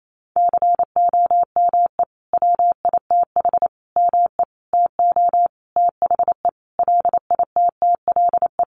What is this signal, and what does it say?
Morse code, heard as a keyed tone, about 28 wpm, 700 Hz. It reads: COME WITH ME TO THE LITTLE
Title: Sentences from Top 200 Words 28wpm